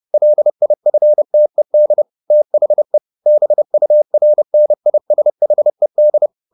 Morse code, heard as a keyed tone, about 30 wpm, 600 Hz. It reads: LIFTED THE BURNISHED
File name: the old man and the sea 2 30wpm